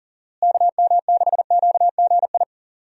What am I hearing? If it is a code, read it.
Morse code, heard as a keyed tone, about 40 wpm, 700 Hz. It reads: KM6QGI
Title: USA Call Signs - 6 Characters - Repeated 2x - 40wpm